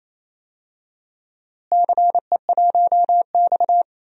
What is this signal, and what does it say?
Morse code, heard as a keyed tone, about 28 wpm, 700 Hz. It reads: CE1X
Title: DXCC-100 Call Signs - 4 Characters - Repeated 2x - 28wpm